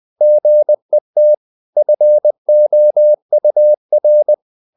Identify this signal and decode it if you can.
Morse code, heard as a keyed tone, about 20 wpm, 600 Hz. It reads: GET FOUR
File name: the old man and the sea / the old man and the sea part 1 20wpm